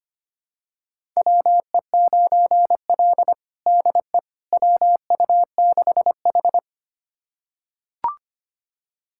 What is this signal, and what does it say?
Morse code, heard as a keyed tone, about 25 wpm, 700 Hz. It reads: WE9L DE WU6H
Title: USA Callsign DE Callsign - 4 Characters - 25wpm